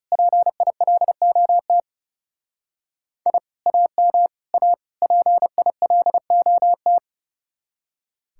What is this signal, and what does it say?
Morse code, heard as a keyed tone, about 38 wpm, 700 Hz. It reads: PILOT I AM A PILOT